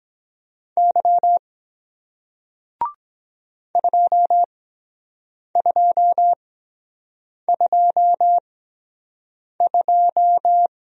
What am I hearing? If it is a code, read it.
Morse code, heard as a keyed tone, about 23 wpm, spaced as for 7 wpm, 700 Hz. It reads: Y 2222